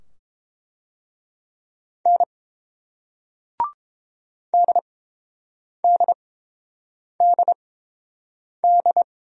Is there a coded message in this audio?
Yes: Morse code from 2.1 to 9.0 s, heard as a keyed tone, about 29 wpm, spaced as for 7 wpm, 700 Hz. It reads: N DDDD